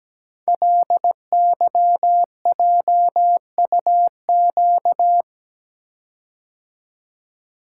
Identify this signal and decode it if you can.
Morse code, heard as a keyed tone, about 17 wpm, 700 Hz. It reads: LYJUQ
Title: Code Groups - 5 Letters - 17wpm